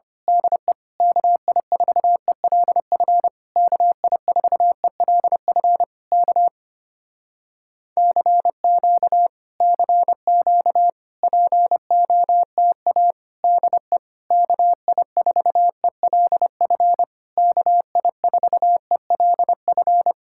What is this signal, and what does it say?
Morse code, heard as a keyed tone, about 26 wpm, 700 Hz. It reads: DE KI4ELF KI4ELF K CQ CQ POTA DE KI4ELF KI4ELF
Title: Speed-Racing Common QSO Phrases Encore - 25wpm